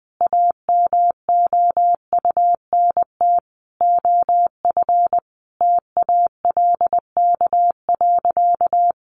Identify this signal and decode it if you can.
Morse code, heard as a keyed tone, about 20 wpm, 700 Hz. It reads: AMOUNT OF TALK.
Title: Sentences from Top 1400 Words 20wpm